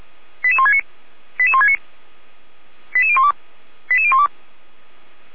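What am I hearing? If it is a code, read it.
Five-tone selective call (ZVEI), heard as quick runs of five tones, tones of about 70 ms. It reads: 88169 88169 88012 88012
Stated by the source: Five-tone ZVEI selective call received by SDR.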